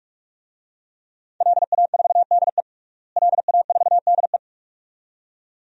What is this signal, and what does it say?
Morse code, heard as a keyed tone, about 46 wpm, 700 Hz. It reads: LA4DE LA4DE
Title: DXCC-100 Call Signs - 5 Characters - Repeated 2x - 45wpm